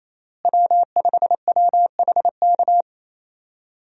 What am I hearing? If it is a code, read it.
Morse code, heard as a keyed tone, about 28 wpm, 700 Hz. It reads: W5WHK
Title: USA Call Signs - 5 Characters - Repeated 3x - 28wpm